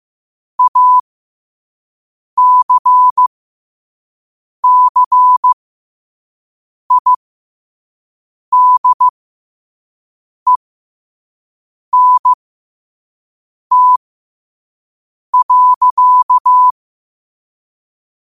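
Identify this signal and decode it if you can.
Morse code, heard as a keyed tone, about 15 wpm, spaced as for 5 wpm, 1000 Hz. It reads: ACCIDENT.
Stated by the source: Morse code sent at 5 wpm.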